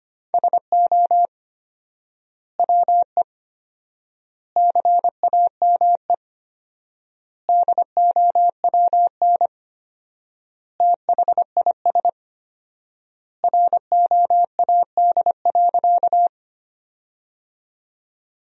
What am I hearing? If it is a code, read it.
Morse code, heard as a keyed tone, about 25 wpm, 700 Hz. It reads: SO WE CAME DOWN THIS ROAD.
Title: 4x Word-Spacing - Sentences from Top 400 Words - 25wpm